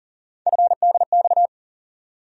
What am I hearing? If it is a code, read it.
Morse code, heard as a keyed tone, about 40 wpm, 700 Hz. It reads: FDX
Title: USA Call Sign Postfixes - 3 Letters 40wpm